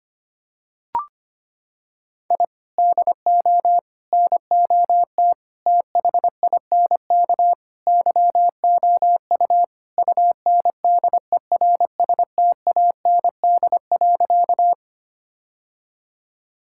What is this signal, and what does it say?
Morse code, heard as a keyed tone, about 25 wpm, 700 Hz. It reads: I DO NOT THINK YOU UNDERSTAND.
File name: practice / 25wpm / Sentences with 4-Word Phrases - 25wpm